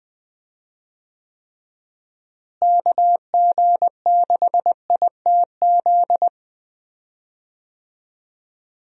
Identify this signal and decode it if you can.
Morse code, heard as a keyed tone, about 20 wpm, 700 Hz. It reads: KG6ITZ
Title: USA Call Signs - 6 Characters - Repeated 2x - 20wpm